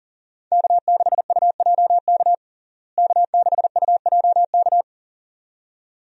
Morse code, heard as a keyed tone, about 40 wpm, 700 Hz. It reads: K6UJK K6UJK
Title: USA Call Signs - 5 Characters - Repeated 2x - 40wpm